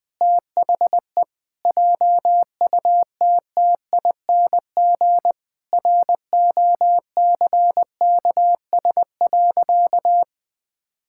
Morse code, heard as a keyed tone, about 20 wpm, 700 Hz. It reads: THE JUTTING ROCKS.
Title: Pangram Challenge v2 - 20wpm